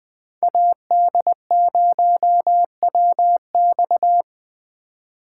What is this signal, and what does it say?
Morse code, heard as a keyed tone, about 20 wpm, 700 Hz. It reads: AD0WX
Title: Speed Racing - USA 2x2 Call Signs 17wpm